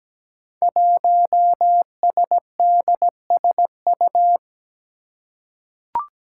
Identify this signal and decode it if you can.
Morse code, heard as a keyed tone, about 17 wpm, 700 Hz. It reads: 1SDSU
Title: Five Challenging Letter-Number Combinations 17wpm